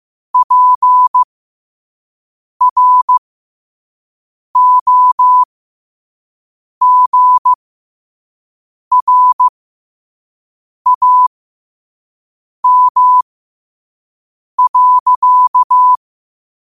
Morse code, heard as a keyed tone, about 15 wpm, spaced as for 5 wpm, 1000 Hz. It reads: PROGRAM.
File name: qotdfiles / 5wpm